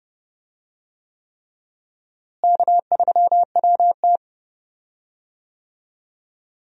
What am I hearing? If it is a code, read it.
Morse code, heard as a keyed tone, about 30 wpm, 700 Hz. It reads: K3WT
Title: USA Call Signs - 4 Characters - Repeated 3x - 30wpm